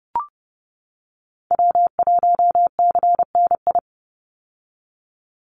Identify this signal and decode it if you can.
Morse code, heard as a keyed tone, about 30 wpm, 700 Hz. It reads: W1CNI
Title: USA Call Signs 30wpm